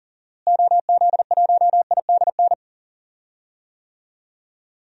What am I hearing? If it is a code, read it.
Morse code, heard as a keyed tone, about 40 wpm, 700 Hz. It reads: OZ1IDN
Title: DXCC-100 Call Signs - 6 Characters - Repeated 3x - 40wpm